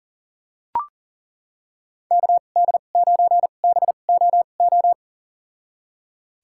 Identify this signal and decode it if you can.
Morse code, heard as a keyed tone, about 40 wpm, spaced as for 29 wpm, 700 Hz. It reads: KD9BOO
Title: USA Call Signs - 6 Characters - 2x Character Spacing - 40wpm